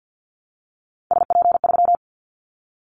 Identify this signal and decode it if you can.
Morse code, heard as a keyed tone, about 48 wpm, 700 Hz. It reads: SP3
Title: Callsign Prefixes for Top 25 DXCC - Repeated 3x - 50wpm